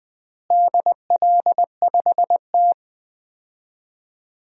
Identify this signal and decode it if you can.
Morse code, heard as a keyed tone, about 20 wpm, 700 Hz. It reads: DL5T
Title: DXCC-100 Call Signs - 4 Characters - Repeated 3x - 20wpm